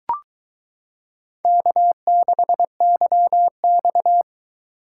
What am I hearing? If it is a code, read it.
Morse code, heard as a keyed tone, about 22 wpm, 700 Hz. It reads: K6YX